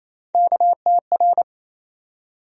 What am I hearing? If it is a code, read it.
Morse code, heard as a keyed tone, about 28 wpm, 700 Hz. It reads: KTR
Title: USA Call Sign Postfixes - 3 Letters 28wpm